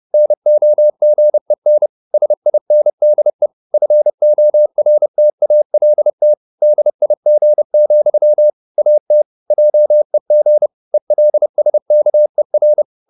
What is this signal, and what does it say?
Morse code, heard as a keyed tone, about 30 wpm, 600 Hz. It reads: NOGEN SINDE FORTALT DIG, AT JEG ELSKER
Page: Ferie 30 wpm